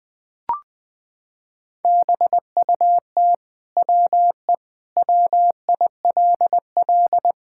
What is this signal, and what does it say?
Morse code, heard as a keyed tone, about 20 wpm, 700 Hz. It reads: BUT WE WILL
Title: Sentences from Top 1000 Words - Review Part 5 - 20wpm